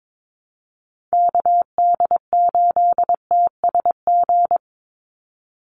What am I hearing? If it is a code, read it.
Morse code, heard as a keyed tone, about 22 wpm, 700 Hz. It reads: KD8TSG
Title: USA Call Signs 22wpm